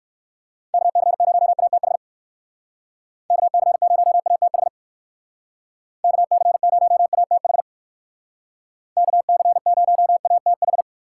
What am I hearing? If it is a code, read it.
Morse code, heard as a keyed tone, about 55 wpm, 700 Hz. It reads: KC0ATH KC0ATH KC0ATH KC0ATH
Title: Speed Racing - USA 2x3 Call Signs 45wpm